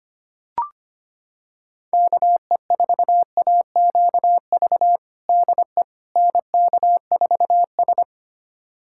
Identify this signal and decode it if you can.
Morse code, heard as a keyed tone, about 25 wpm, 700 Hz. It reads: KE4AQV DE NK4S
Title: USA Callsign DE Callsign 25wpm